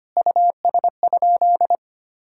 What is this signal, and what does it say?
Morse code, heard as a keyed tone, about 25 wpm, 700 Hz. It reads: US?
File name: practice / 25wpm / Sentences with 3-Word Phrases - 25wpm